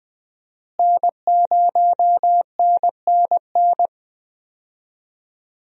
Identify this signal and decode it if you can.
Morse code, heard as a keyed tone, about 21 wpm, 700 Hz. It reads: N0NNN
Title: USA Call Signs - 5 Characters - Repeated 3x - 20wpm